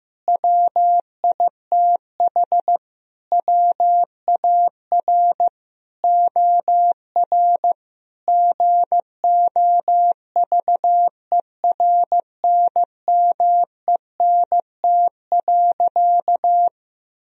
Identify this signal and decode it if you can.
Morse code, heard as a keyed tone, about 15 wpm, 700 Hz. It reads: WITH WAR OR GOVERNMENT.
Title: Sentences from Top 1000 Words - Review Part 3 - 15wpm